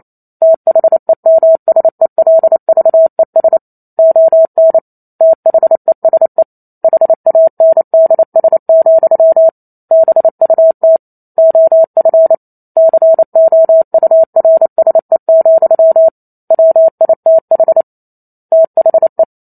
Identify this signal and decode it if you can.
Morse code, heard as a keyed tone, about 28 wpm, 650 Hz. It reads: THEMSELVES ON THESE HANDS, BUT OF COURSE, WITH THE
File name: Chapter 2 - 28 WPM